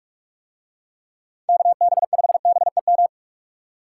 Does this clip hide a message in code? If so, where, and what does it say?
Yes: Morse code from 1.5 to 3.1 s, heard as a keyed tone, about 44 wpm, 700 Hz. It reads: KB5BEM